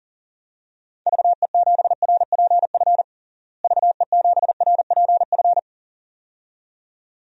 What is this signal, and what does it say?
Morse code, heard as a keyed tone, about 40 wpm, 700 Hz. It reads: VE7RPF VE7RPF
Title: DXCC-100 Call Signs - 6 Characters - Repeated 2x - 40wpm